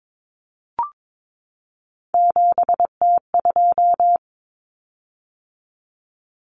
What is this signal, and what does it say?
Morse code, heard as a keyed tone, about 22 wpm, 700 Hz. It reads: 7T2
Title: Common Cut Numbers - 3-Digit 22wpm